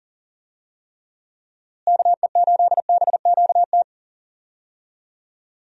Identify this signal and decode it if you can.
Morse code, heard as a keyed tone, about 40 wpm, 700 Hz. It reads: KE8BQT
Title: USA Call Signs - 6 Characters - Repeated 2x - 40wpm